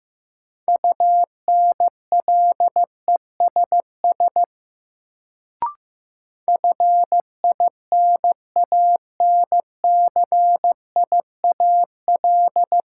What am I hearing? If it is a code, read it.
Morse code, heard as a keyed tone, about 15 wpm, 700 Hz. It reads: UNLESS FINANCIAL
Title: Sentences from Top 1000 Words - Review Part 5 - New Word Drill - 15wpm